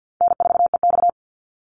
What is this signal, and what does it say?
Morse code, heard as a keyed tone, about 50 wpm, 700 Hz. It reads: N4EX
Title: NA SOTA Activator Callsigns 50wpm